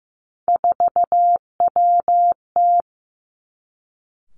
Morse code, heard as a keyed tone, about 15 wpm, 700 Hz. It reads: 4WT
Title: Three Challenging Letter-Number Combinations 15wpm